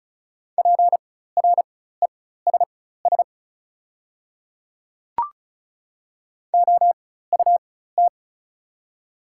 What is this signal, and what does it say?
Morse code, heard as a keyed tone, about 36 wpm, spaced as for 16 wpm, 700 Hz. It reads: PRESS OUT
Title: 4x Character Spacing - Top 300 Words - 35wpm